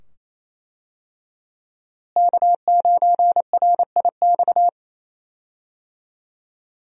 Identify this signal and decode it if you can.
Morse code, heard as a keyed tone, about 28 wpm, 700 Hz. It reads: K9RIX
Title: USA Call Signs - 5 Characters - Repeated 2x - 28wpm